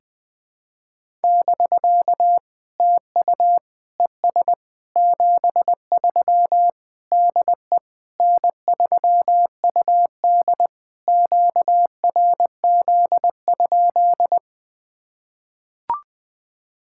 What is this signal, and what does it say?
Morse code, heard as a keyed tone, about 20 wpm, 700 Hz. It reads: <BK> TU ES 73 DE N3UD QRZ?
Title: POTA QSO Practice - 20wpm